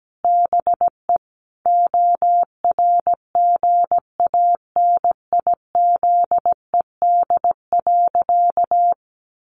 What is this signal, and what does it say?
Morse code, heard as a keyed tone, about 17 wpm, 700 Hz. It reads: BE ORGANIZED.
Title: Sentences from Top 1300 Words 17wpm